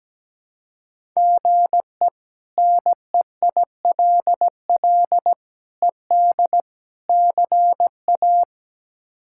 Morse code, heard as a keyed tone, about 17 wpm, 700 Hz. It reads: GE NEILL ED CA
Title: K1USN SST QSO Practice - 17wpm